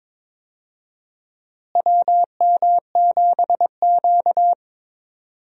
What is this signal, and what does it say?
Morse code, heard as a keyed tone, about 22 wpm, 700 Hz. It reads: WM7Q
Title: USA Call Signs - 4 Characters - Repeated 3x - 22wpm